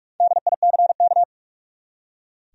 Morse code, heard as a keyed tone, about 44 wpm, 700 Hz. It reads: DICK
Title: USA Common Names - Max 5 Letters - 45wpm